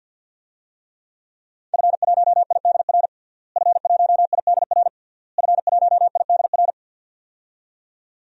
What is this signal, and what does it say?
Morse code, heard as a keyed tone, about 49 wpm, 700 Hz. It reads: F1IDR F1IDR F1IDR
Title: DXCC-100 Call Signs - 5 Characters - Repeated 3x - 50wpm